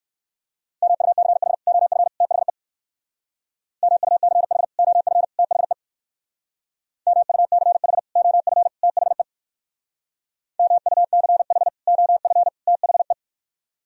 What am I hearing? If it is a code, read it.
Morse code, heard as a keyed tone, about 62 wpm, 700 Hz. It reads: MUCH OF THE MUCH OF THE MUCH OF THE MUCH OF THE